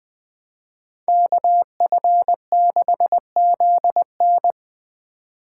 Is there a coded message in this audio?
Yes: Morse code from 1.1 to 4.5 s, heard as a keyed tone, about 21 wpm, 700 Hz. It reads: KF6ZN